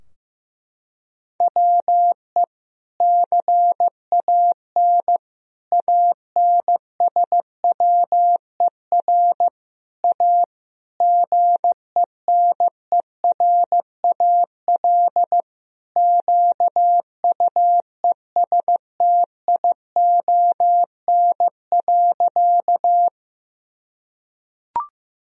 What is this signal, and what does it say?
Morse code, heard as a keyed tone, about 15 wpm, 700 Hz. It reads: WE CAN ANSWER A GENERAL QUESTION.
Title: Sentences from Top 1000 Words - Review Part 3 - 15wpm